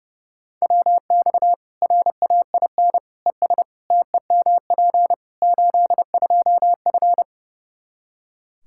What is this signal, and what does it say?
Morse code, heard as a keyed tone, about 30 wpm, 700 Hz. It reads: WX RAIN ES TEMP 82F
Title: Practice QSO - Exchange 2 - Info - 30wpm